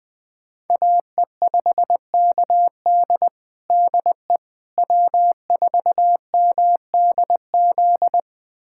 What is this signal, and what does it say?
Morse code, heard as a keyed tone, about 20 wpm, 700 Hz. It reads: AE5KD DE W4MDZ
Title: USA Callsign DE Callsign - 5 Characters - 20wpm